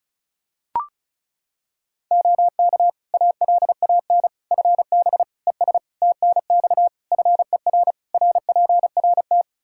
Morse code, heard as a keyed tone, about 34 wpm, 700 Hz. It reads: OK ALAN FB ES TNX FER RPRT
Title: Practice QSO - Exchange 2 - Info - 35wpm